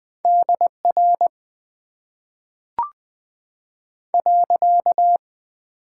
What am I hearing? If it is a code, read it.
Morse code, heard as a keyed tone, about 20 wpm, spaced as for 5 wpm, 700 Hz. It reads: DR.
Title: Top 90 QSO Elements - 8x Character Spacing - 20wpm